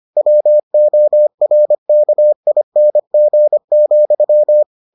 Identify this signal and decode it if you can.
Morse code, heard as a keyed tone, about 25 wpm, 600 Hz. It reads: WORKING,